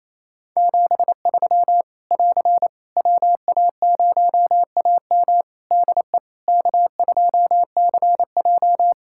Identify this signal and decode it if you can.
Morse code, heard as a keyed tone, about 28 wpm, 700 Hz. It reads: 73 <AR> WA0AM DE K2CJ
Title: K1USN SST QSO Practice 28wpm